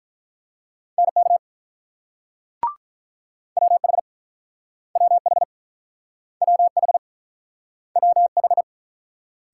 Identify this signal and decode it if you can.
Morse code, heard as a keyed tone, about 48 wpm, 700 Hz. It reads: NK WH WH WH WH